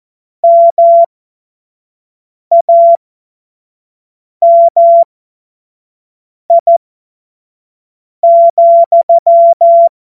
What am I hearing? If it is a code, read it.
Morse code, heard as a keyed tone, about 14 wpm, spaced as for 5 wpm, 680 Hz. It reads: MAMI,
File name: Morsekurs / Witze / Witz 25. Mai 2021, 25 F70